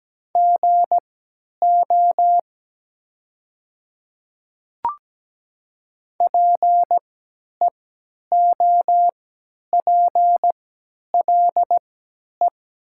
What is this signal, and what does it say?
Morse code, heard as a keyed tone, about 17 wpm, spaced as for 10 wpm, 700 Hz. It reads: GO PEOPLE
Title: 3x Character Spacing - Top 100 Words - 17wpm